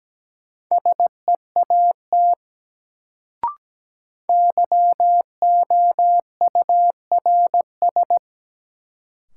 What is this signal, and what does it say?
Morse code, heard as a keyed tone, about 17 wpm, 700 Hz. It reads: SEAT YOURS
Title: Sentences from Top 1500 Words - New Word Drill Alt - 17wpm